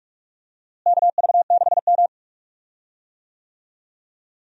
Morse code, heard as a keyed tone, about 46 wpm, 700 Hz. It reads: KV6M
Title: USA Call Signs - 4 Characters - Repeated 3x - 45wpm